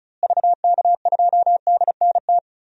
Morse code, heard as a keyed tone, about 34 wpm, 700 Hz. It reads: VK2DNT
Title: DXCC-100 Call Signs - 6 Characters - Repeated 3x - 35wpm